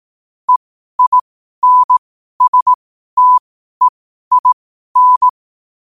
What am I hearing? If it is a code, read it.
Morse code, heard as a keyed tone, about 18 wpm, spaced as for 12 wpm, 1000 Hz. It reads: EINSTEIN
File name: 12wpm